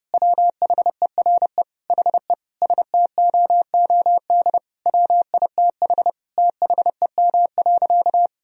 Morse code, heard as a keyed tone, about 32 wpm, 700 Hz. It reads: WHERE HE STOOD WITH THEM.
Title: Sentences from Top 1000 Words - Review Part 2 - 30wpm